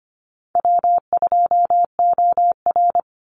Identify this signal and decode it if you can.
Morse code, heard as a keyed tone, about 25 wpm, 700 Hz. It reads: W2OR
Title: Easy Call Signs - 25wpm